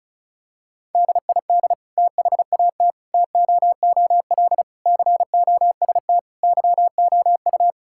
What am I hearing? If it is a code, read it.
Morse code, heard as a keyed tone, about 34 wpm, 700 Hz. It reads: DID THAT TOOL COST YOU
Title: Sentences from Top 1000 Words - Review Part 6 - 35wpm